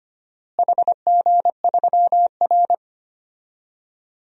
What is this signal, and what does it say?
Morse code, heard as a keyed tone, about 25 wpm, 700 Hz. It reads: HG3R
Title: DXCC-100 Call Signs - 4 Characters - Repeated 2x - 25wpm